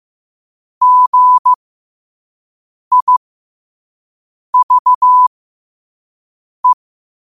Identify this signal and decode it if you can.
Morse code, heard as a keyed tone, about 15 wpm, spaced as for 5 wpm, 1000 Hz. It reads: GIVE
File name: qotdfiles / 5wpm